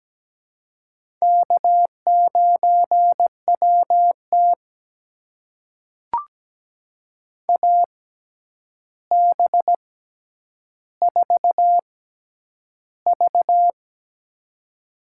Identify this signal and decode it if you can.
Morse code, heard as a keyed tone, about 17 wpm, spaced as for 9 wpm, 700 Hz. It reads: K9WT AB4V
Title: USA Call Signs - 4 Characters - 6x Character Spacing - 17wpm